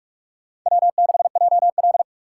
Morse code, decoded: W6JL

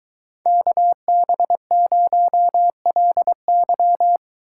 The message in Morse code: KB0LY

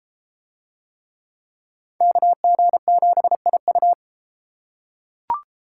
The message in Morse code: KG7IU

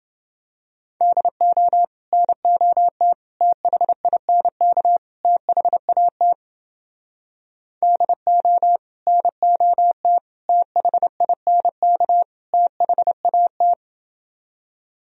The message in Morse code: DO NOT THINK THAT DO NOT THINK THAT

